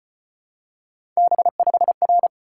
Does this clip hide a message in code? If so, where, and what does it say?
Yes: Morse code from 1.2 to 2.3 s, heard as a keyed tone, about 35 wpm, 700 Hz. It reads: B5R